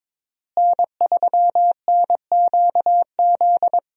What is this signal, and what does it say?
Morse code, heard as a keyed tone, about 23 wpm, 700 Hz. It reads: N3NQZ
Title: USA Call Signs - 5 Characters - Repeated 3x - 22wpm